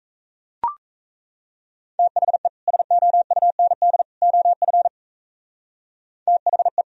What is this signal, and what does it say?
Morse code, heard as a keyed tone, about 33 wpm, 700 Hz. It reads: THE SOUND OF THE
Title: Speed-Racing - Sets of 3 Words 28wpm